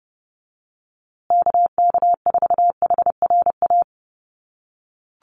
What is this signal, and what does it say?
Morse code, heard as a keyed tone, about 30 wpm, 700 Hz. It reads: KK4HRA
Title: USA Call Signs Repeated 2x 30wpm